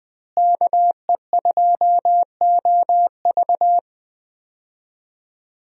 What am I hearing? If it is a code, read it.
Morse code, heard as a keyed tone, about 21 wpm, 700 Hz. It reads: KE2OV
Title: USA Call Signs - 5 Characters - Repeated 3x - 20wpm